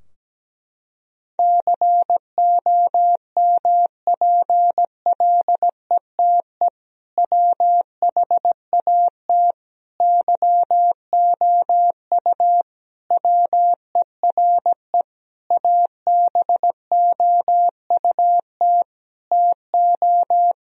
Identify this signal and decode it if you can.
Morse code, heard as a keyed tone, about 17 wpm, 700 Hz. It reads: COMPLETE WHAT YOU WERE ABOUT TO
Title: Sentences from Top 1000 Words - Review Part 2 - 17wpm